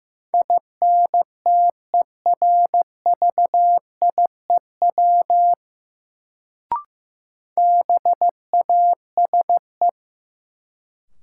Morse code, heard as a keyed tone, about 15 wpm, 700 Hz. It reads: INTERVIEW BASE
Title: Sentences from Top 1600 Words - New Word Drill Alt - 15wpm